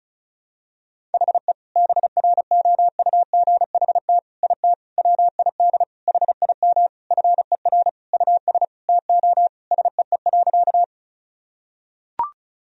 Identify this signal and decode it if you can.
Morse code, heard as a keyed tone, about 36 wpm, 700 Hz. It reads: HE BROUGHT IT WID HIM FER US TO SEE.
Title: Sentences from Top 500 Words and Abbreviations 35wpm